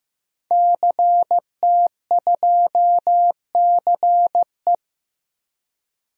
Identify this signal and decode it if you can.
Morse code, heard as a keyed tone, about 15 wpm, 700 Hz. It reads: CT2CE
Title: DXCC-100 Call Signs - 5 Characters - Repeated 3x - 15wpm